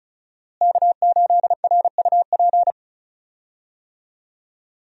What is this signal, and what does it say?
Morse code, heard as a keyed tone, about 34 wpm, 700 Hz. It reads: K8RUP
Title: USA Call Signs - 5 Characters - Repeated 3x - 35wpm